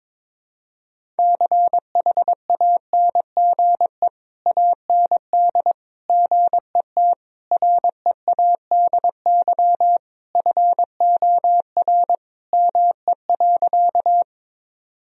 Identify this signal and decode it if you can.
Morse code, heard as a keyed tone, about 22 wpm, 700 Hz. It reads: CHANGE AND GET READY FOR ME.
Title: Sentences from Top 1000 Words - Review Part 1 - 22wpm